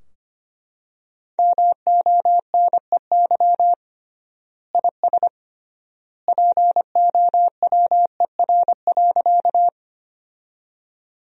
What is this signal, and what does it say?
Morse code, heard as a keyed tone, about 25 wpm, 700 Hz. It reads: MONEY IS POWER.
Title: 3x Word-Spacing - Sentences from Top 500 Words - 25wpm